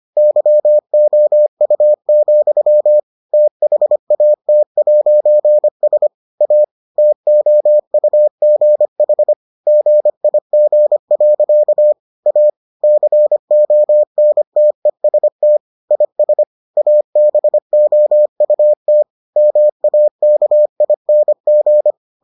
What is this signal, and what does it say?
Morse code, heard as a keyed tone, about 25 wpm, 600 Hz. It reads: YOU, THAT'S A TOUGH GIG. A CONTEST IS ABOUT MAKING